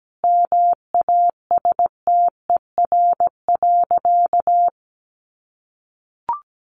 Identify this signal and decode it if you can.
Morse code, heard as a keyed tone, about 17 wpm, 700 Hz. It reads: MASTER.
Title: Sentences from Top 1300 Words 17wpm